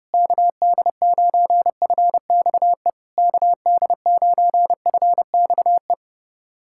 Morse code, heard as a keyed tone, about 30 wpm, 690 Hz. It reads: KD9FXE KD9FXE
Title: SOTA QSO Practice 30wpm